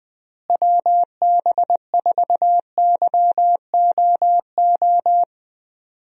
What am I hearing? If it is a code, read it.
Morse code, heard as a keyed tone, about 21 wpm, 700 Hz. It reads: WB4YOO
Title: USA Call Signs - 6 Characters - Repeated 3x - 20wpm